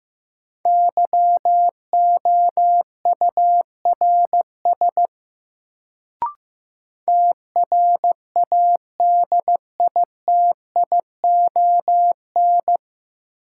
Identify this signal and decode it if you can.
Morse code, heard as a keyed tone, about 16 wpm, 700 Hz. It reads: YOURS TRADITION
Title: Sentences from Top 1500 Words - New Word Drill - 15wpm